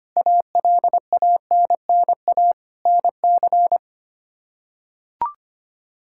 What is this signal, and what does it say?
Morse code, heard as a keyed tone, about 25 wpm, 700 Hz. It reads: ALANNA NC
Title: K1USN SST QSO Practice - 25wpm